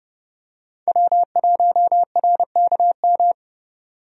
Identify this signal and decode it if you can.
Morse code, heard as a keyed tone, about 32 wpm, 700 Hz. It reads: W1RKM
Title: USA Call Signs - 5 Characters - Repeated 2x - 30wpm